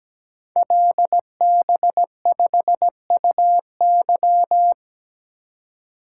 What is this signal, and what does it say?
Morse code, heard as a keyed tone, about 17 wpm, 690 Hz. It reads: LB5UY
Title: Code Groups - 5 Letter:Numbers - 17wpm